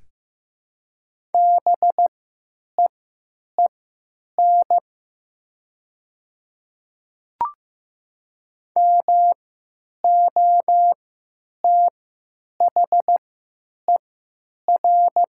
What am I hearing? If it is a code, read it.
Morse code, heard as a keyed tone, about 15 wpm, spaced as for 9 wpm, 700 Hz. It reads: BEEN MOTHER
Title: 3x Character Spacing - Top 200 Words - 15wpm